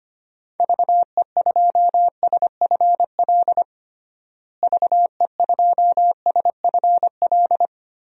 Morse code, heard as a keyed tone, about 25 wpm, 700 Hz. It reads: VE2SFL VE2SFL
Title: DXCC-100 Call Signs - 6 Characters - Repeated 3x - 25wpm